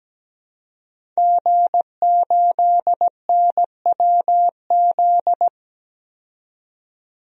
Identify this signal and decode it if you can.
Morse code, heard as a keyed tone, about 17 wpm, 700 Hz. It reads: G8NWZ
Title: DXCC-100 Call Signs - 5 Characters - 17wpm